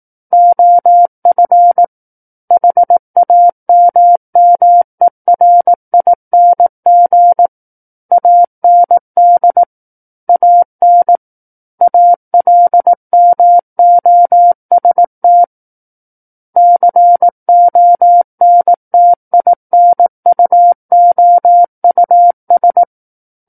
Morse code, heard as a keyed tone, about 18 wpm, 700 Hz. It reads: OF HAMMERING AND AN ALMOST CONTINUOUS
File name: Book One / 09-Chapter 9 - 18 WPM